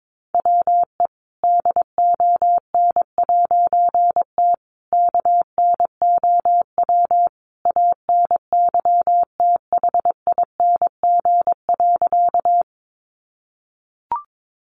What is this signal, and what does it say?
Morse code, heard as a keyed tone, about 22 wpm, 700 Hz. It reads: WE DON'T KNOW ANYTHING.
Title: A Sound Of Thunder 22wpm